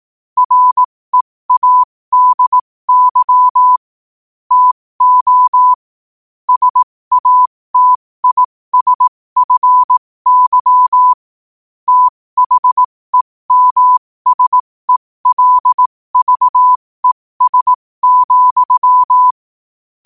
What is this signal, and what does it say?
Morse code, heard as a keyed tone, about 18 wpm, spaced as for 15 wpm, 1000 Hz. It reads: READY TO SATISFY THEMSELVES,
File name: qotdfiles / 15wpm